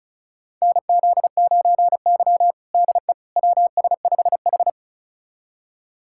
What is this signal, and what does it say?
Morse code, heard as a keyed tone, about 36 wpm, 700 Hz. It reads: NZ9Y DE WS5H
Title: USA Callsign DE Callsign - 4 Characters - 35wpm